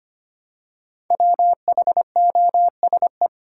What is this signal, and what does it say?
Morse code, heard as a keyed tone, about 25 wpm, 700 Hz. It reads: WHOSE
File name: 5 Letter Words - First 3 Letters Given - 25wpm